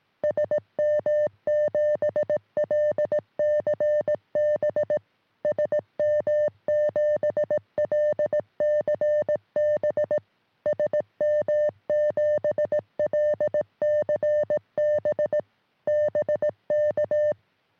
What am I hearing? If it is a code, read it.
Morse code, heard as a keyed tone, about 18 wpm, 600 Hz. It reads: SM7LCB SM7LCB SM7LCB BK
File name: sm7lcb / g4bao all 23cm